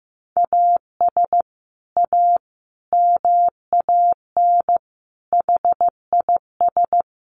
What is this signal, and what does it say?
Morse code, heard as a keyed tone, about 15 wpm, 700 Hz. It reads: AS A MAN HIS
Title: Sentences from Top 1200 Words 15wpm